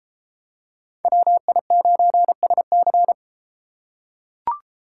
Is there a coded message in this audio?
Yes: Morse code from 1.1 to 3.1 s, heard as a keyed tone, about 33 wpm, 700 Hz. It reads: WI9SC